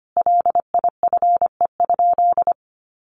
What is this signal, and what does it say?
Morse code, heard as a keyed tone, about 25 wpm, 700 Hz. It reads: LIFE?
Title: Sentences from Top 1500 Words 25wpm